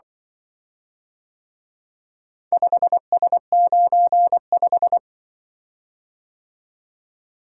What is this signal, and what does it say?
Morse code, heard as a keyed tone, about 24 wpm, 700 Hz. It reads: 5S95
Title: Speed-Racing - Four Letter-Number Combinations - 20wpm